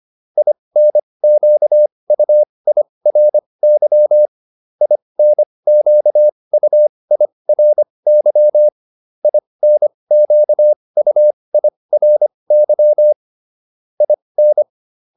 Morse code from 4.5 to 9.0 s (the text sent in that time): INQUIRY